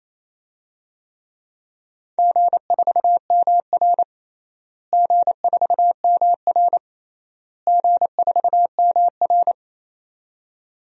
Morse code, heard as a keyed tone, about 27 wpm, 700 Hz. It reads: G4MR G4MR G4MR
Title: DXCC-100 Call Signs - 4 Characters - Repeated 3x - 28wpm